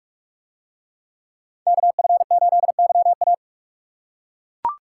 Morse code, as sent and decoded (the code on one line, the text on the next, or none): -.- ..-. ---.. -.-- .-
KF8YA